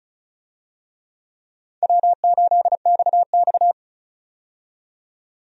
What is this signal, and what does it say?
Morse code, heard as a keyed tone, about 34 wpm, 700 Hz. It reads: W8XX
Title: USA Call Signs - 4 Characters - 35wpm